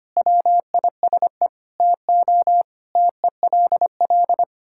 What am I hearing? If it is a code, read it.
Morse code, heard as a keyed tone, about 25 wpm, 700 Hz. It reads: WISE TO TELL
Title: Sentences from Top 1600 Words 25wpm